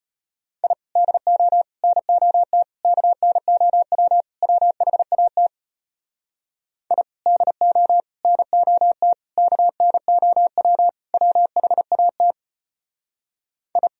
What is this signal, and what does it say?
Morse code, heard as a keyed tone, about 33 wpm, 700 Hz. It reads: I DO NOT KNOW WHAT I DO NOT KNOW WHAT I